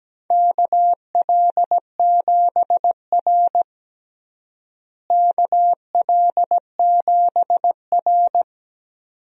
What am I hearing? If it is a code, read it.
Morse code, heard as a keyed tone, about 17 wpm, 700 Hz. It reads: KL7R KL7R
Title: DXCC-100 Call Signs - 4 Characters - Repeated 2x - 17wpm